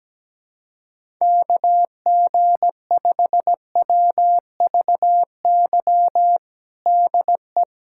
Morse code, heard as a keyed tone, about 17 wpm, 700 Hz. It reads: KG5WVY DE
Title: USA Callsign DE Callsign - 6 Characters - 17wpm